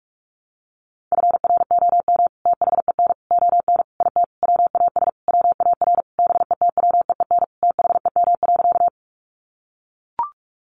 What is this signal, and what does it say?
Morse code, heard as a keyed tone, about 44 wpm, 700 Hz. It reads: FROM THEN ON IT WAS WAR BETWEEN THEM.
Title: Sentences from Top 400 Words 45wpm